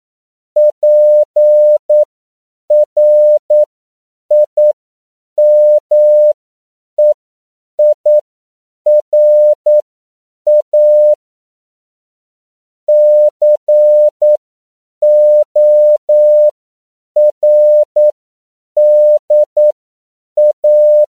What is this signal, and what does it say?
Morse code, heard as a keyed tone, about 9 wpm, spaced as for 7 wpm, 600 Hz. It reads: PRIMEIRA CORDA
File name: CW / 10PPM